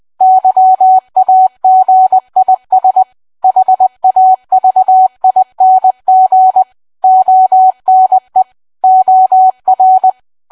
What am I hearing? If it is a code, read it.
Morse code, heard as a keyed tone, about 20 wpm, 750 Hz. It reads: YAGIS HAVING ONE OR